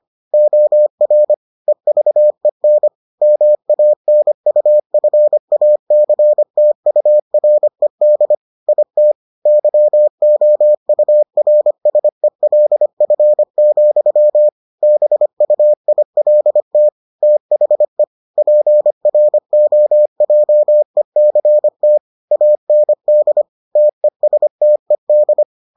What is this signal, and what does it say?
Morse code, heard as a keyed tone, about 25 wpm, 600 Hz. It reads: OR EVEN MANUFACTURED IT YOURSELF, BUILT THE PROJECT AND TESTED